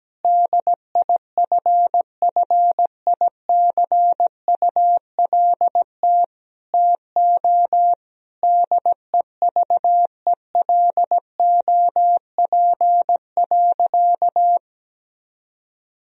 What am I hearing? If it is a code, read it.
Morse code, heard as a keyed tone, about 17 wpm, 700 Hz. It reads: DIFFICULT TO DEVELOP.
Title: Sentences from Top 1000 Words - Review Part 2 - 17wpm